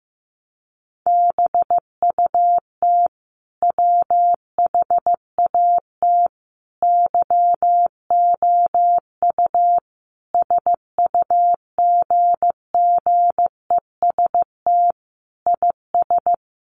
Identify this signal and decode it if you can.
Morse code, heard as a keyed tone, about 15 wpm, 700 Hz. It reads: BUT WHAT YOU SUGGEST IS
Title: Sentences from Top 1300 Words 15wpm